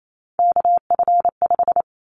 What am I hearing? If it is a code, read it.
Morse code, heard as a keyed tone, about 28 wpm, 700 Hz. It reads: KF5
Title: Callsign Prefixes for Top 25 DXCC - 28wpm